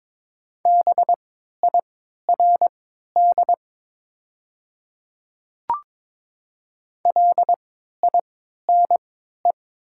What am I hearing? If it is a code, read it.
Morse code, heard as a keyed tone, about 22 wpm, spaced as for 13 wpm, 700 Hz. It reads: BIRD LINE
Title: 3x Character Spacing - Top 500 Words - 22wpm